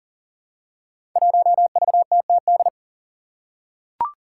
Morse code, heard as a keyed tone, about 37 wpm, 700 Hz. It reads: 1VTTD
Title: Five Challenging Letter-Number Combinations 40wpm